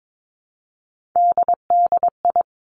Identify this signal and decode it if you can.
Morse code, heard as a keyed tone, about 22 wpm, 700 Hz. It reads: DDI
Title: Three Challenging Letter-Number Combinations 22wpm